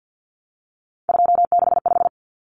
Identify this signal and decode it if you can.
Morse code, heard as a keyed tone, about 50 wpm, 700 Hz. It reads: 265
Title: Numbers - 3-Digit 50wpm